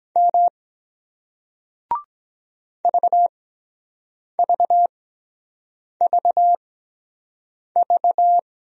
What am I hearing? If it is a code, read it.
Morse code, heard as a keyed tone, about 23 wpm, spaced as for 7 wpm, 700 Hz. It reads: M VVVV